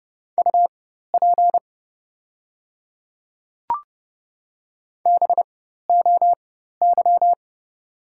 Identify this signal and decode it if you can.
Morse code, heard as a keyed tone, about 30 wpm, spaced as for 14 wpm, 700 Hz. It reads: UP BOY
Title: 4x Character Spacing - Top 200 Words - 30wpm